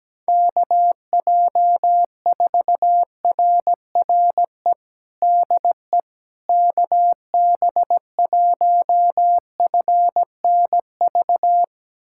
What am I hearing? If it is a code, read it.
Morse code, heard as a keyed tone, about 17 wpm, 700 Hz. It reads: KJ4RRE DE KB1FNV
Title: USA Callsign DE Callsign - 6 Characters - 17wpm